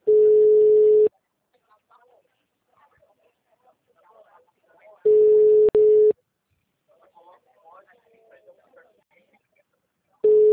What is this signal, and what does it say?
Ringback tone (Europe, 425 Hz), with speech.